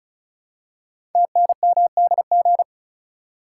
Morse code, heard as a keyed tone, about 36 wpm, 690 Hz. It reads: TNMDG
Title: Code Groups - 5 Letters - 35wpm